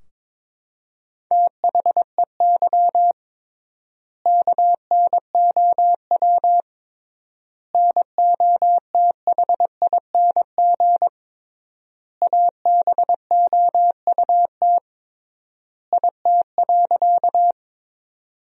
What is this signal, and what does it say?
Morse code, heard as a keyed tone, about 22 wpm, 700 Hz. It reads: THEY KNOW NOTHING ABOUT IT.
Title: 3x Word-Spacing - Sentences from Top 500 Words - 22wpm